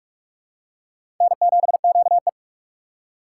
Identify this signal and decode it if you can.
Morse code, heard as a keyed tone, about 44 wpm, 700 Hz. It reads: N7QE
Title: USA Call Signs - 4 Characters - 4x Character Spacing - 45wpm